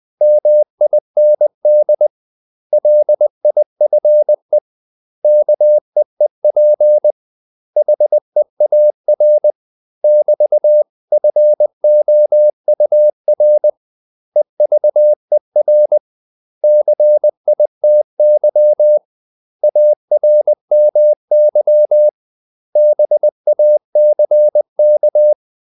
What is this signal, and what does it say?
Morse code, heard as a keyed tone, about 20 wpm, 600 Hz. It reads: MIND LIFE KEEP HEAR <BT>FOUR EVER CITY ARMY BACK
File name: CW / Practice / 500MostCommonEnglishWords20